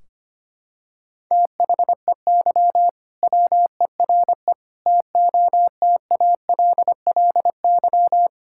Morse code, heard as a keyed tone, about 25 wpm, 700 Hz. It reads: THEY WERE TOTALLY
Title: Sentences from Top 1600 Words 25wpm